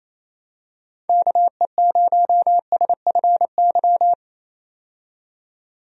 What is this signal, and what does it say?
Morse code, heard as a keyed tone, about 28 wpm, 700 Hz. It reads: KE0SFY
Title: USA Call Signs - 6 Characters - 28wpm